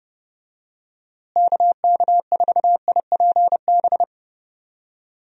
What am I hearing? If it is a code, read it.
Morse code, heard as a keyed tone, about 30 wpm, 700 Hz. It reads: KK4IPB
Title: USA Call Signs - 6 Characters - Repeated 2x - 30wpm